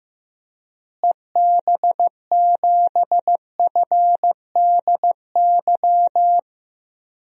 Morse code, heard as a keyed tone, about 15 wpm, 700 Hz. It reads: EB7FDY
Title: DXCC-100 Call Signs - 6 Characters - Repeated 3x - 15wpm